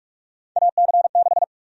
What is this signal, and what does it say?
Morse code, heard as a keyed tone, about 46 wpm, 700 Hz. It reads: AC6
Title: USA Call Sign Prefixes - 45wpm